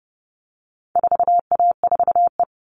Morse code, heard as a keyed tone, about 30 wpm, 700 Hz. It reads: 4A4E
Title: All Cut Numbers - 4-Digit 30wpm